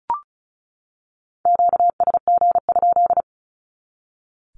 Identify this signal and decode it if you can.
Morse code, heard as a keyed tone, about 34 wpm, 700 Hz. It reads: QSG?
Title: Q-Codes 35wpm